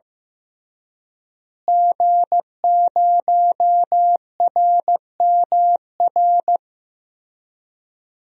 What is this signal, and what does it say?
Morse code, heard as a keyed tone, about 16 wpm, 700 Hz. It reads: G0RMR
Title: DXCC-100 Call Signs - 5 Characters - Repeated 3x - 15wpm